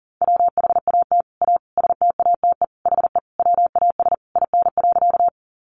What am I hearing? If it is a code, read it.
Morse code, heard as a keyed tone, about 38 wpm, 700 Hz. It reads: WHAT A STATE HE WAS IN.
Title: Sentences from Top 400 Words 40wpm